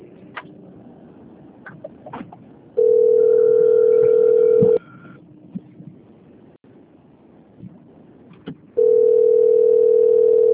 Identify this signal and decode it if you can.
Ringback tone (North America), 440 + 480 Hz.
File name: out- - - - - .99363